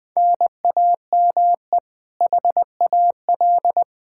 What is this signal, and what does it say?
Morse code, heard as a keyed tone, about 20 wpm, 700 Hz. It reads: NAME HAL
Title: Practice QSO - Exchange 1 - Beginning - 20wpm